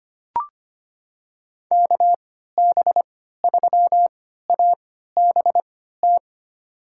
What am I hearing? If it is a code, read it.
Morse code, heard as a keyed tone, about 25 wpm, spaced as for 14 wpm, 700 Hz. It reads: KB3ABT
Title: USA Call Signs - 3x Character Spacing - 25wpm